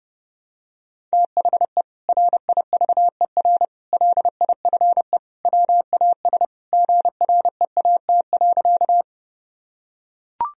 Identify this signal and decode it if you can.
Morse code, heard as a keyed tone, about 30 wpm, 700 Hz. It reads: THE RIVER LIFE WAS GREAT.
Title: Sentences from Top 1000 Words - Review Part 1 - 30wpm